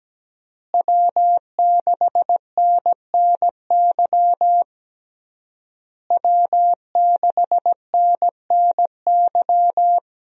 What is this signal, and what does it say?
Morse code, heard as a keyed tone, about 17 wpm, 700 Hz. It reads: W6NNY W6NNY
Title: USA Call Signs - 5 Characters - Repeated 3x - 17wpm